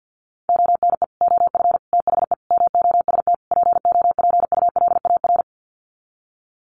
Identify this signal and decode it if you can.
Morse code, heard as a keyed tone, about 50 wpm, 700 Hz. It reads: ONE OF THE MOST POPULAR
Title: Sets of 5 Words 50wpm